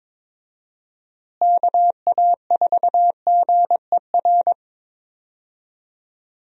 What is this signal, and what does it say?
Morse code, heard as a keyed tone, about 22 wpm, 700 Hz. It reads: KA4GER
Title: USA Call Signs - 6 Characters - Repeated 2x - 22wpm